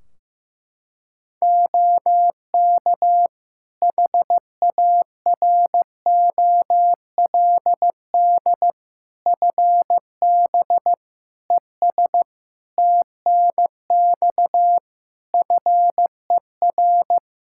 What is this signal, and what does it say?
Morse code, heard as a keyed tone, about 15 wpm, 700 Hz. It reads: OK HAROLD FB ES TNX FER
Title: Practice QSO - Exchange 2 - Info - 15wpm